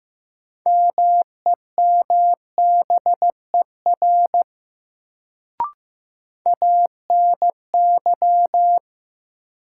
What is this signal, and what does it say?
Morse code, heard as a keyed tone, about 15 wpm, 700 Hz. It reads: MEMBER ANY
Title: Sets of 3 Words - New Word Drill - 15wpm